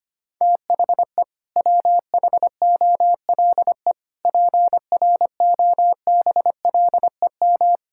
Morse code, heard as a keyed tone, about 25 wpm, 700 Hz. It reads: THE WHOLE PROBLEM
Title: Sentences from Top 1000 Words - Review Part 6 - 25wpm